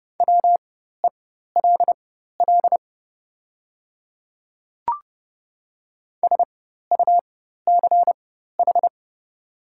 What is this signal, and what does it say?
Morse code, heard as a keyed tone, about 30 wpm, spaced as for 14 wpm, 700 Hz. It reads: WELL SUCH
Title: 4x Character Spacing - Top 200 Words - 30wpm